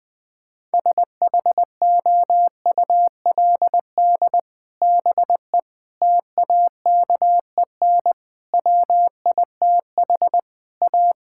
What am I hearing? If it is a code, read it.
Morse code, heard as a keyed tone, about 20 wpm, 700 Hz. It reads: SHOULD BE TAKEN WITH A
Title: Sets of 8 Words - 20wpm